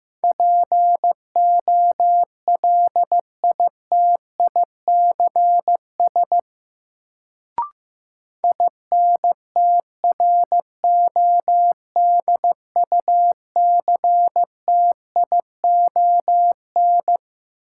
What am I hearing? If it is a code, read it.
Morse code, heard as a keyed tone, about 15 wpm, 700 Hz. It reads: POLITICS INTRODUCTION
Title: Sentences from Top 1000 Words - New Word Drill - 15wpm